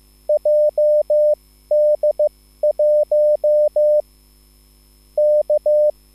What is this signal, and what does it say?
Morse code, heard as a keyed tone, about 15 wpm, spaced as for 13 wpm, 600 Hz. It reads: JD1 K
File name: QSL / CW lesson